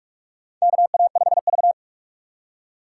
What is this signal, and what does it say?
Morse code, heard as a keyed tone, about 46 wpm, 700 Hz. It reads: KA5V
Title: USA Call Signs - 4 Characters - 2x Character Spacing - 45wpm